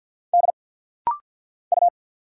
Morse code, heard as a keyed tone, about 50 wpm, 690 Hz. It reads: D U
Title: Single Letter-Number Alt - Rapid-Fire 50wpm